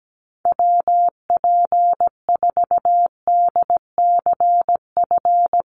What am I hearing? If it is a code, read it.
Morse code, heard as a keyed tone, about 17 wpm, 700 Hz. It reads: WP4DCF
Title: USA Call Signs Repeated 2x 17wpm